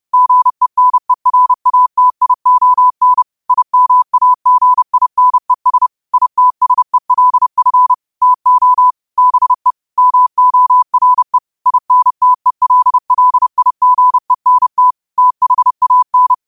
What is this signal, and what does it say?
Morse code, heard as a keyed tone, about 30 wpm, 1000 Hz. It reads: GENERATION IMAGINES ITSELF TO BE MORE INTELLIGENT THAN
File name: qotdfiles / 30wpm